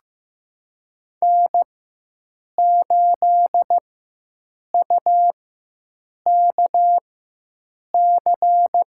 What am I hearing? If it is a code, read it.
Morse code, heard as a keyed tone, about 16 wpm, spaced as for 7 wpm, 700 Hz. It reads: N8UKC
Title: USA Call Signs - 5 Characters - 4x Character Spacing - 15wpm